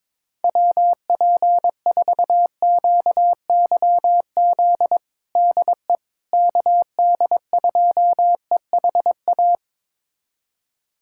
Morse code, heard as a keyed tone, about 22 wpm, 700 Hz. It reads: WP4QYZ DE KD2EHA
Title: USA Callsign DE Callsign - 6 Characters - 22wpm